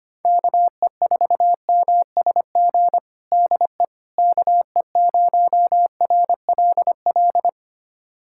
Morse code, heard as a keyed tone, about 25 wpm, 700 Hz. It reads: KE4MSG DE KE0RLL
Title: USA Callsign DE Callsign - 6 Characters - 25wpm